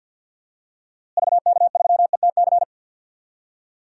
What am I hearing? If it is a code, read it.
Morse code, heard as a keyed tone, about 51 wpm, 700 Hz. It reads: VK3ETC
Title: DXCC-100 Call Signs - 6 Characters - Repeated 3x - 50wpm